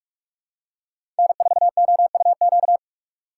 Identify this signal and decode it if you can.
Morse code, heard as a keyed tone, about 41 wpm, 700 Hz. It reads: N4OUQ
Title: USA Call Signs - 5 Characters - 3x Character Spacing - 45wpm